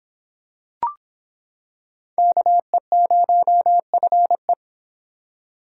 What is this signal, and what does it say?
Morse code, heard as a keyed tone, about 26 wpm, 700 Hz. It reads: KE0FE